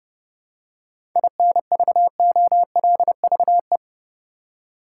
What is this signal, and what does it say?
Morse code, heard as a keyed tone, about 30 wpm, 700 Hz. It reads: INVOLVE
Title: Seven Letter Words - 30wpm